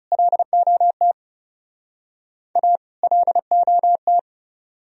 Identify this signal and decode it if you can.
Morse code, heard as a keyed tone, about 38 wpm, 700 Hz. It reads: LOT A LOT